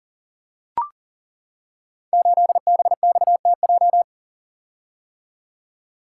Morse code, heard as a keyed tone, about 42 wpm, 700 Hz. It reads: 8BXTJ
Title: Code Groups - 5 Letter:Numbers - 40wpm